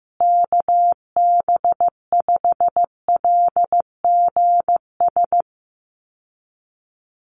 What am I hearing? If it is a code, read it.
Morse code, heard as a keyed tone, about 15 wpm, 700 Hz. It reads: KB5LGS
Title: USA Call Signs Repeated 3x 15wpm